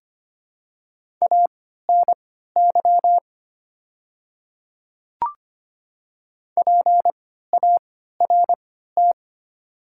Morse code, heard as a keyed tone, about 25 wpm, spaced as for 14 wpm, 700 Hz. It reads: ANY PART
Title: 3x Character Spacing - Top 200 Words - 25wpm